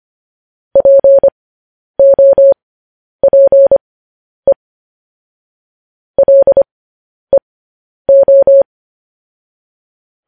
Morse code, heard as a keyed tone, about 25 wpm, spaced as for 10 wpm, 550 Hz. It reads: POPE LEO
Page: News Headlines in Morse Code at 10 WPM